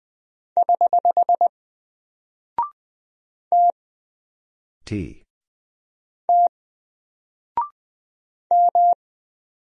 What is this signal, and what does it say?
Morse code, heard as a keyed tone, about 20 wpm, 700 Hz. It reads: <HH> T T M
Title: Single Character (Letter:Number:Punctuation:Prosign) 20wpm